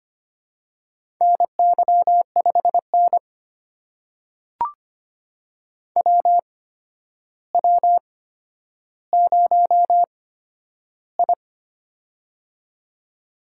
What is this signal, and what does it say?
Morse code, heard as a keyed tone, about 25 wpm, spaced as for 11 wpm, 700 Hz. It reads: NY5N WW0I
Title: USA Call Signs - 4 Characters - 8x Character Spacing - 25wpm